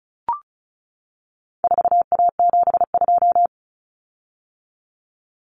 Morse code, heard as a keyed tone, about 34 wpm, 700 Hz. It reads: 4A72
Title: Common Cut Numbers - 4-Digit 35wpm